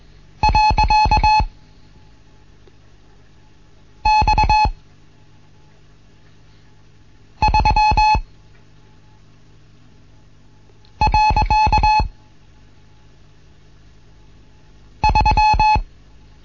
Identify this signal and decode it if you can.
Morse code, heard as a keyed tone, about 21 wpm, 860 Hz. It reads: . X 3 . 3
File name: Morsen / Kurs A 15